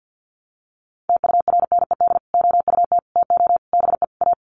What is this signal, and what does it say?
Morse code, heard as a keyed tone, about 50 wpm, 700 Hz. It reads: TURNED OUT TO BE A
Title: Sets of 5 Words 50wpm